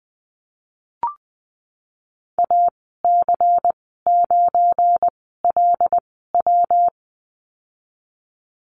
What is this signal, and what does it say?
Morse code, heard as a keyed tone, about 20 wpm, spaced as for 15 wpm, 700 Hz. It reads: AC9LW
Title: 193 Lesson 31g - Letter v - Call Signs 20wpm